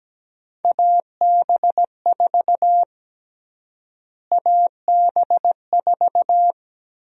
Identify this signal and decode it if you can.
Morse code, heard as a keyed tone, about 17 wpm, 700 Hz. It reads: AB4 AB4
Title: USA Call Sign Prefixes - Repeated 2x - 17wpm